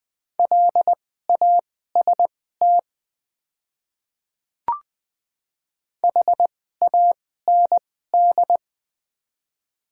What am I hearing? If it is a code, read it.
Morse code, heard as a keyed tone, about 20 wpm, spaced as for 14 wpm, 700 Hz. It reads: LAST HAND
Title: 2x Character Spacing - Top 300 Words - 20wpm